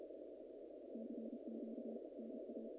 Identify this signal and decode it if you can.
Morse code, heard as a keyed tone, about 32 wpm, 240 Hz. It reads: CQ NA
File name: contests / naqp-cw-k5tr-op-k5pi- -2100